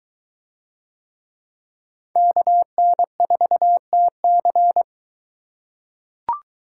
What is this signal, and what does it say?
Morse code, heard as a keyed tone, about 23 wpm, 700 Hz. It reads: KN4TC